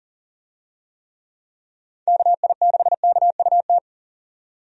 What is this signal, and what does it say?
Morse code, heard as a keyed tone, about 40 wpm, 700 Hz. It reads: KI6KUT
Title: USA Call Signs - 6 Characters - 40wpm